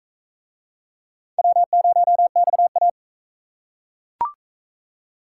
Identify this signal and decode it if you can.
Morse code, heard as a keyed tone, about 36 wpm, 700 Hz. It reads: W0XA